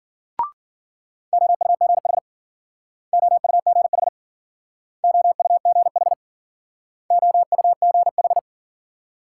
Morse code, heard as a keyed tone, about 51 wpm, 700 Hz. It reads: OUGH OUGH OUGH OUGH